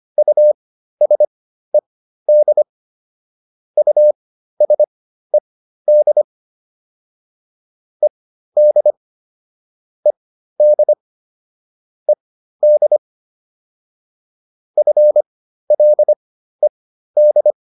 Morse code, heard as a keyed tone, about 25 wpm, spaced as for 13 wpm, 600 Hz. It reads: USED USED ED ED ED FLED